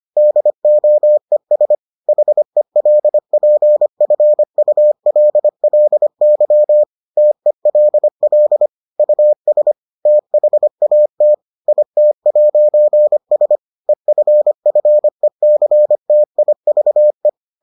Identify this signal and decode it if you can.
Morse code, heard as a keyed tone, about 25 wpm, 600 Hz. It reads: DOES HELPFULLY TELL US THAT IT'S EFFECTIVE